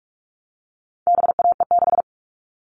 Morse code, heard as a keyed tone, about 44 wpm, 700 Hz. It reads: BAE6
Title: All Cut Numbers - 4-Digit 45wpm